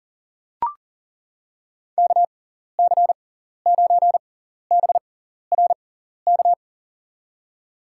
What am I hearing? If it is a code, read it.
Morse code, heard as a keyed tone, about 40 wpm, spaced as for 14 wpm, 700 Hz. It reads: KC9BRK
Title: USA Call Signs - 6 Characters - 6x Character Spacing - 40wpm